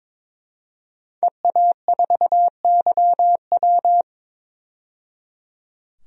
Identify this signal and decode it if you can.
Morse code, heard as a keyed tone, about 21 wpm, 700 Hz. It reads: EA4YW
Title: DXCC-100 Call Signs - 5 Characters - 22wpm